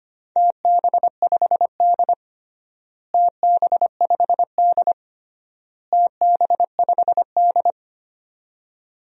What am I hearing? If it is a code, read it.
Morse code, heard as a keyed tone, about 25 wpm, 700 Hz. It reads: TB5D TB5D TB5D
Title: DXCC-100 Call Signs - 4 Characters - Repeated 3x - 25wpm